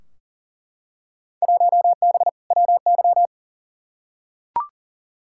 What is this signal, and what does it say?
Morse code, heard as a keyed tone, about 40 wpm, 700 Hz. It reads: 1B WY
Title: Field Day Exchanges Encore - 40wpm